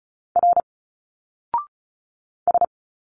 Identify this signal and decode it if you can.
Morse code, heard as a keyed tone, about 34 wpm, 700 Hz. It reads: R S
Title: Single Letters 35wpm